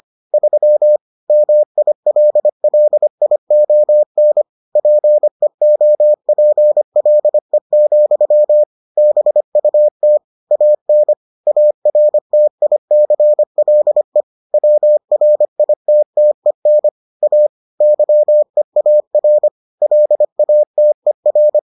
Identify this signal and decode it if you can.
Morse code, heard as a keyed tone, about 25 wpm, 600 Hz. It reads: 3 MILLION PEOPLE, BUT AN ARTICLE WRITTEN A YEAR LATER